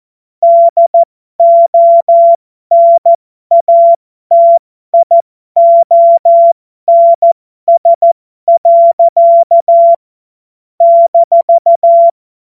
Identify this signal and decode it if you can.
Morse code, heard as a keyed tone, about 14 wpm, spaced as for 12 wpm, 680 Hz. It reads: DONATIONS. -
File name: Morsekurs / Witze